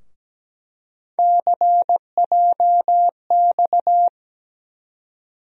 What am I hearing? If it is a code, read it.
Morse code, heard as a keyed tone, about 17 wpm, 700 Hz. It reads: CJX
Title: USA Call Sign Postfixes - 3 Letters 17wpm